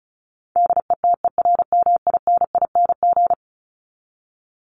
Morse code, heard as a keyed tone, about 34 wpm, 700 Hz. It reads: DETERMINING
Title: ING Suffix - Top 100 Words with Base Words 35wpm